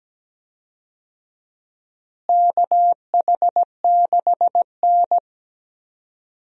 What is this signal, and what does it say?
Morse code, heard as a keyed tone, about 17 wpm, 700 Hz. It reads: KH6N
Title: DXCC-100 Call Signs - 4 Characters - Repeated 3x - 17wpm